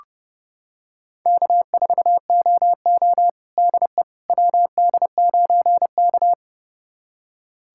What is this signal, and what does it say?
Morse code, heard as a keyed tone, about 30 wpm, 700 Hz. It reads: K4OO DE WD9K
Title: USA Callsign DE Callsign - 4 Characters - 30wpm